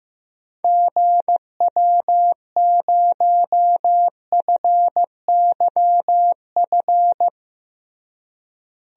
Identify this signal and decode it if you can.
Morse code, heard as a keyed tone, about 15 wpm, 700 Hz. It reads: GW0FYF
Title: DXCC-100 Call Signs - 6 Characters - Repeated 3x - 15wpm